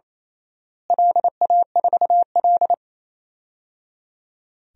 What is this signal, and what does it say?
Morse code, heard as a keyed tone, about 28 wpm, 700 Hz. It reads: LA4L
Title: DXCC-100 Call Signs - 4 Characters - Repeated 3x - 28wpm